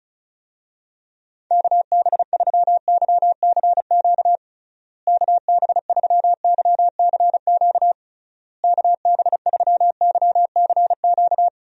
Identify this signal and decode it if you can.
Morse code, heard as a keyed tone, about 34 wpm, 700 Hz. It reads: KB3YCQ KB3YCQ KB3YCQ
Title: USA Call Signs - 6 Characters - Repeated 3x - 35wpm